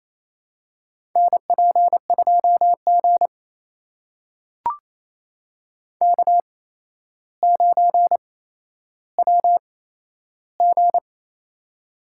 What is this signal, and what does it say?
Morse code, heard as a keyed tone, about 27 wpm, spaced as for 12 wpm, 700 Hz. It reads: NP2G K9WG
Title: USA Call Signs - 4 Characters - 8x Character Spacing - 28wpm